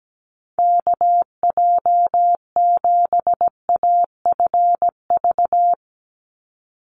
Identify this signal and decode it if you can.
Morse code, heard as a keyed tone, about 17 wpm, 700 Hz. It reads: KJ7AFV
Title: USA Call Signs 17wpm